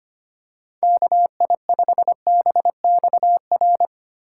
Morse code, heard as a keyed tone, about 28 wpm, 700 Hz. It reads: KI5BXR
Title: Speed Racing - USA 2x3 Call Signs 25wpm